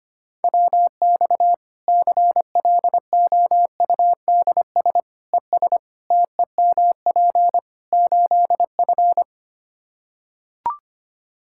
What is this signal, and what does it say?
Morse code, heard as a keyed tone, about 25 wpm, 700 Hz. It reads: WX CLOUDS ES TEMP 8F
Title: Practice QSO - Exchange 2 - Info - 25wpm